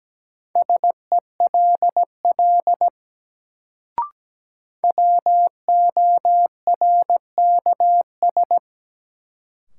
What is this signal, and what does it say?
Morse code, heard as a keyed tone, about 17 wpm, 700 Hz. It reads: SELL WORKS
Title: Sentences from Top 1100 Words - New Word Drill Alt - 17wpm